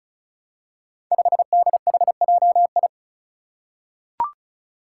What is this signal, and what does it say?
Morse code, heard as a keyed tone, about 36 wpm, 700 Hz. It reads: 5DHJI
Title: Five Challenging Letter-Number Combinations 35wpm